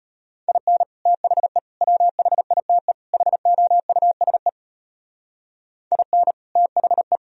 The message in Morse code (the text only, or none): IN THE WHITE HOUSE IN THE